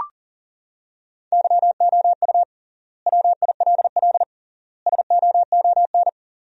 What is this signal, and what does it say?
Morse code, heard as a keyed tone, about 40 wpm, 700 Hz. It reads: YOU WILL SOON
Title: 3x Word-Spacing - Sentences from Top 400 Words - 40wpm